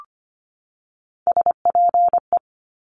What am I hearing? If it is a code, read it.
Morse code, heard as a keyed tone, about 25 wpm, 700 Hz. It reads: SPE
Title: Top Three Letter Combinations in Words 25wpm